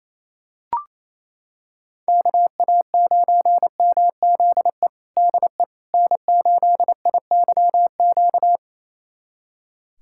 Morse code, heard as a keyed tone, about 28 wpm, 700 Hz. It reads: KA9MZE DE N8IYQ
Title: USA Callsign DE Callsign 28wpm